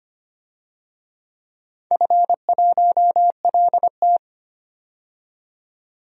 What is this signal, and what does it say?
Morse code, heard as a keyed tone, about 25 wpm, 700 Hz. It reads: F1LT
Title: DXCC-100 Call Signs - 4 Characters - 25wpm